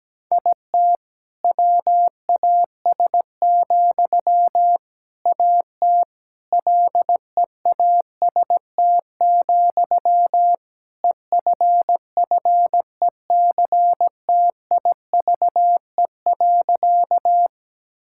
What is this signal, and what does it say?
Morse code, heard as a keyed tone, about 17 wpm, 700 Hz. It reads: IT WAS, AT LEAST, EFFECTIVE.
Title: Sentences from Top 1000 Words - Review Part 6 - 17wpm